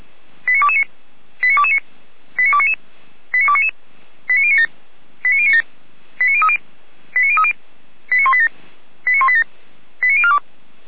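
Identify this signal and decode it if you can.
Five-tone selective call (ZVEI), heard as quick runs of five tones, tones of about 70 ms. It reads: 89229 89229 89299 89299 89097 89097 89030 89030 89187 89187 89042
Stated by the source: ZVEI five-tone selective call received by SDR, from the Feuerwehr Orlamünde fire brigade server.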